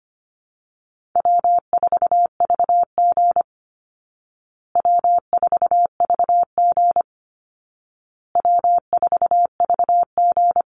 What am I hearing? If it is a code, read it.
Morse code, heard as a keyed tone, about 25 wpm, 700 Hz. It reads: W4VG W4VG W4VG
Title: Easy Call Signs Repeated 3x - 25wpm